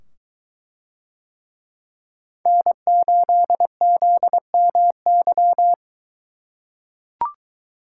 Morse code, heard as a keyed tone, about 22 wpm, 700 Hz. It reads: N8ZMY